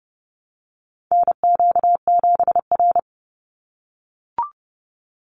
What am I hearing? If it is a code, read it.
Morse code, heard as a keyed tone, about 30 wpm, 700 Hz. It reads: NQ7R
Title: NA SOTA Chaser Callsigns 30wpm